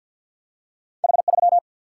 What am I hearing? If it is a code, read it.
Morse code, heard as a keyed tone, about 53 wpm, 700 Hz. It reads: H 3
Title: Single-Letter Single-Number 50wpm